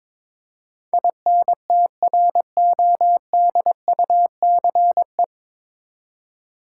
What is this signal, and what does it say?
Morse code, heard as a keyed tone, about 22 wpm, 700 Hz. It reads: INTRODUCE
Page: Morse Code Ninja practice 22wpm